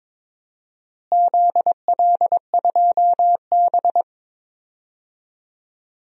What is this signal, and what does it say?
Morse code, heard as a keyed tone, about 21 wpm, 700 Hz. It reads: ZL2B
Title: DXCC-100 Call Signs - 4 Characters - Repeated 2x - 22wpm